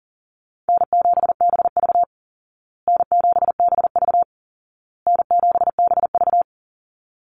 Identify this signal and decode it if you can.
Morse code, heard as a keyed tone, about 40 wpm, 700 Hz. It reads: N7BV N7BV N7BV
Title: Call Signs Repeated 2x - 40wpm